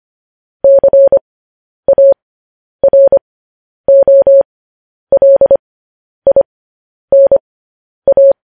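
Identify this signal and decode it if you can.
Morse code, heard as a keyed tone, about 25 wpm, spaced as for 10 wpm, 550 Hz. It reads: CAROLINA
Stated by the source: Morse code news headlines transmission.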